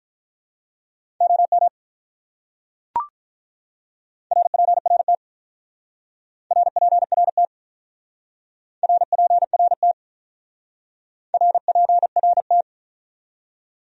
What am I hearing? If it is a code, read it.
Morse code, heard as a keyed tone, about 45 wpm, 700 Hz. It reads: OM RPRT RPRT RPRT RPRT